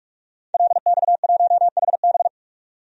Morse code, heard as a keyed tone, about 46 wpm, 700 Hz. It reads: LX1HB
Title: DXCC-100 Call Signs - 5 Characters - 45wpm